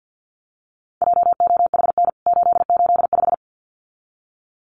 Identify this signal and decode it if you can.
Morse code, heard as a keyed tone, about 49 wpm, 700 Hz. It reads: JOHN 885
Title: CWT Exchange 50wpm